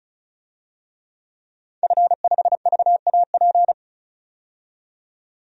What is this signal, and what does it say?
Morse code, heard as a keyed tone, about 36 wpm, 700 Hz. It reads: F5VAP
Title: DXCC-100 Call Signs - 5 Characters - 35wpm